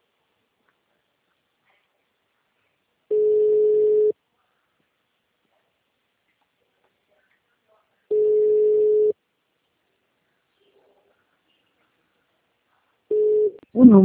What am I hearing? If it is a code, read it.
Ringback tone (Europe, 425 Hz), with speech.
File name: out- -2040- - - .77